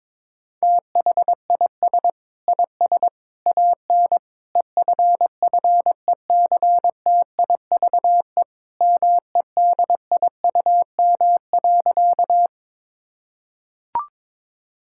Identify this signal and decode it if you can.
Morse code, heard as a keyed tone, about 22 wpm, 700 Hz. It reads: THIS IS AN EFFECTIVE MEDIUM.
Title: Sentences from Top 1000 Words - Review Part 6 - 22wpm